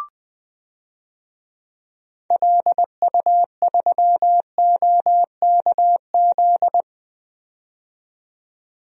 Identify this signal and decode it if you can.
Morse code, heard as a keyed tone, about 20 wpm, 700 Hz. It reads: LU3OKZ
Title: DXCC-100 Call Signs - 6 Characters - 20wpm